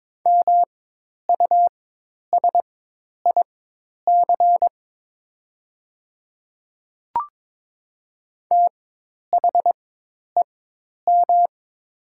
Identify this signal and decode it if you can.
Morse code, heard as a keyed tone, about 22 wpm, spaced as for 10 wpm, 700 Hz. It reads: MUSIC THEM
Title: 4x Character Spacing - Top 300 Words - 22wpm